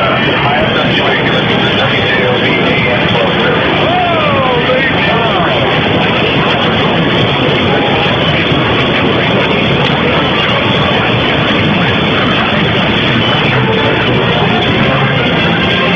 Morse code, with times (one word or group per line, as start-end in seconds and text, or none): none